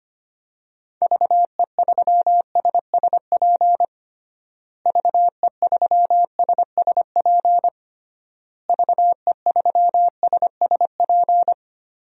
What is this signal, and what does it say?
Morse code, heard as a keyed tone, about 25 wpm, 700 Hz. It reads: VE3SSP VE3SSP VE3SSP
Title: DXCC-100 Call Signs - 6 Characters - Repeated 3x - 25wpm